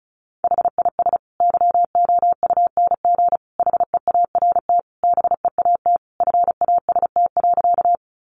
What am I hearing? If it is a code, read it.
Morse code, heard as a keyed tone, about 34 wpm, 700 Hz. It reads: HIS YOUNG HEART BEAT FAST.
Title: Sentences from Top 1300 Words 35wpm